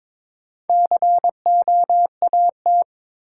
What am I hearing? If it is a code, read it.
Morse code, heard as a keyed tone, about 22 wpm, 700 Hz. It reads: COAT
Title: Sentences from Top 1600 Words - New Word Drill Alt - 22wpm